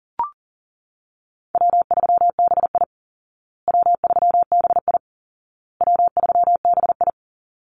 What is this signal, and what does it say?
Morse code, heard as a keyed tone, about 40 wpm, 700 Hz. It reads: W3BI W3BI W3BI
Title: Call Signs Repeated 2x - 40wpm